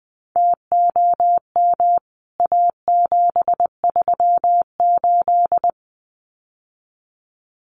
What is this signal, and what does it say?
Morse code, heard as a keyed tone, about 20 wpm, 700 Hz. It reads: TOM A738
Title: CWT Exchange 20wpm